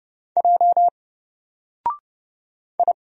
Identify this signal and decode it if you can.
Morse code, heard as a keyed tone, about 30 wpm, 700 Hz. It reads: J I
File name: Single-Letter-Number-30wpm